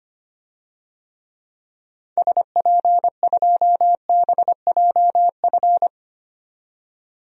Morse code, heard as a keyed tone, about 25 wpm, 700 Hz. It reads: SP2BJF
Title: DXCC-100 Call Signs - 6 Characters - 25wpm